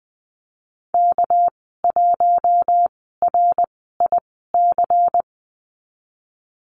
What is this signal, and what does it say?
Morse code, heard as a keyed tone, about 20 wpm, spaced as for 15 wpm, 700 Hz. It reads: K1RIC
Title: 207 Lesson 33g - Letter j - Call Signs 20wpm